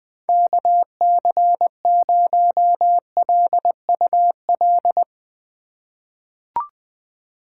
Morse code, heard as a keyed tone, about 20 wpm, 700 Hz. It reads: KC0LUL
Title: USA Call Signs - 6 Characters - 4x Character Spacing - 20wpm